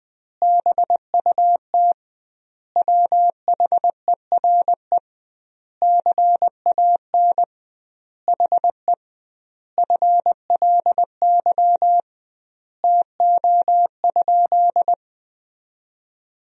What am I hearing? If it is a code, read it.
Morse code, heard as a keyed tone, about 21 wpm, 700 Hz. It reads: BUT WHERE CAN HE FLY TO?
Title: 2x Word-Spacing - Sentences from Top 500 Words - 20wpm